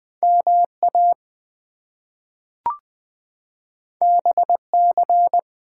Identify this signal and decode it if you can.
Morse code, heard as a keyed tone, about 20 wpm, 700 Hz. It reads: MA BC
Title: US States and Canadian Provinces 20wpm